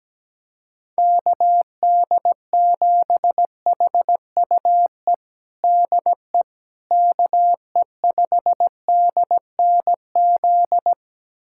Morse code, heard as a keyed tone, about 17 wpm, 700 Hz. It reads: KD7HUE DE KE5DNZ
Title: USA Callsign DE Callsign - 6 Characters - 17wpm